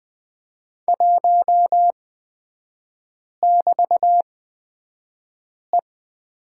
Morse code, heard as a keyed tone, about 20 wpm, spaced as for 5 wpm, 700 Hz. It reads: 1<BT>E
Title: 224 Lesson 36e - BT - Three Characters 20wpm